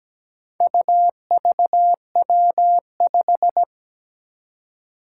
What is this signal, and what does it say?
Morse code, heard as a keyed tone, about 17 wpm, 700 Hz. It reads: UVW5
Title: Four Challenging Letter-Number Combinations 17wpm